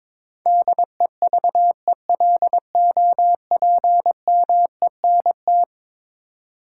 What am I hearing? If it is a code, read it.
Morse code, heard as a keyed tone, about 22 wpm, 700 Hz. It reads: DEVELOPMENT
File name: practice / 22wpm / Eleven Letter Words - 22wpm